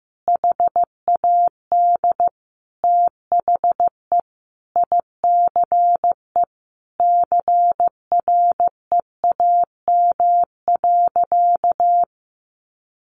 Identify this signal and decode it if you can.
Morse code, heard as a keyed tone, about 15 wpm, 700 Hz. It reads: HAD THE ICE CREAM.
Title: Sentences from Top 1500 Words 15wpm